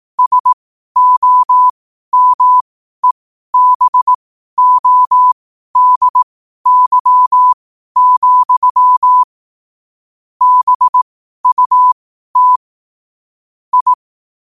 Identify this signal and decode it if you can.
Morse code, heard as a keyed tone, about 18 wpm, spaced as for 12 wpm, 1000 Hz. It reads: SOMEBODY, BUT I
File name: qotdfiles / 12wpm